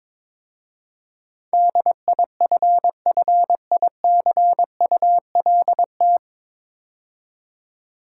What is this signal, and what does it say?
Morse code, heard as a keyed tone, about 23 wpm, 700 Hz. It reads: DIFFICULT
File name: 9 Letter Words - First 3 Letters Given - 22wpm